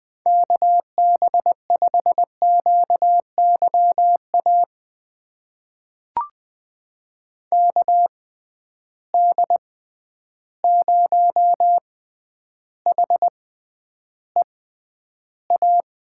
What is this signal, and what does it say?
Morse code, heard as a keyed tone, about 20 wpm, spaced as for 10 wpm, 700 Hz. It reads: KB5QYA KD0HEA
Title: USA Call Signs - 6 Characters - 6x Character Spacing - 20wpm